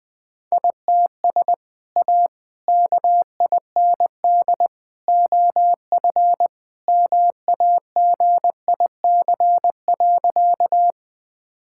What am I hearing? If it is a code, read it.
Morse code, heard as a keyed tone, about 21 wpm, 700 Hz. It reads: ITS A KIND OF MAGIC.
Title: Sentences with 3-Word Phrases - 20wpm